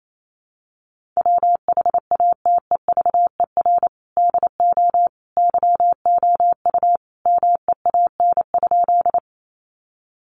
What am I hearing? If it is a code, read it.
Morse code, heard as a keyed tone, about 28 wpm, 700 Hz. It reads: WHATEVER DO YOU MEAN?
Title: Sentences from Top 1100 Words 28wpm